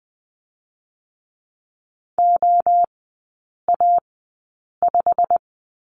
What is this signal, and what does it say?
Morse code, heard as a keyed tone, about 20 wpm, spaced as for 8 wpm, 700 Hz. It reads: OA5
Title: 087 Lesson 15e - Number 5 - Three Characters 20wpm